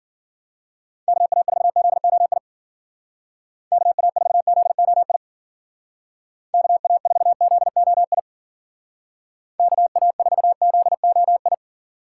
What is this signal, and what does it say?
Morse code, heard as a keyed tone, about 49 wpm, 700 Hz. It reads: KA4ZOI KA4ZOI KA4ZOI KA4ZOI
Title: Speed Racing - USA 2x3 Call Signs 40wpm